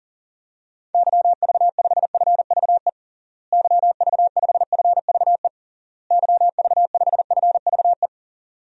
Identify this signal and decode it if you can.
Morse code, heard as a keyed tone, about 40 wpm, 700 Hz. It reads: YV5FVE YV5FVE YV5FVE
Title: DXCC-100 Call Signs - 6 Characters - Repeated 3x - 40wpm